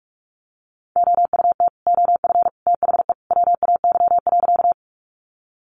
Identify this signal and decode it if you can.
Morse code, heard as a keyed tone, about 44 wpm, 700 Hz. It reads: OUT OF THE WAY.
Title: Sentences from Top 100 Words 45wpm